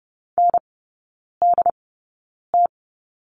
Morse code, heard as a keyed tone, about 30 wpm, spaced as for 9 wpm, 700 Hz. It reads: NDT
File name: 081 Lesson 14e - Number 2 - Three Characters 30wpm